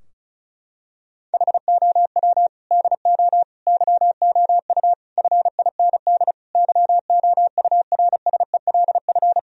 Morse code, heard as a keyed tone, about 36 wpm, 700 Hz. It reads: HOW DO YOU FIND YOURSELF
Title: Sentences from Top 1000 Words - Review Part 4 - 35wpm